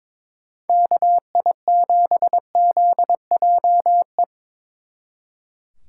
Morse code, heard as a keyed tone, about 26 wpm, 700 Hz. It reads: KI7ZJE
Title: Speed Racing - USA 2x3 Call Signs 22wpm